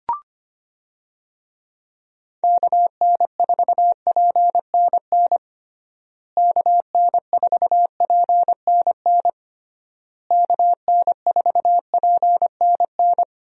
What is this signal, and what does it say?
Morse code, heard as a keyed tone, about 25 wpm, 700 Hz. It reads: KN4PNN KN4PNN KN4PNN
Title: USA Call Signs - 6 Characters - Repeated 3x - 25wpm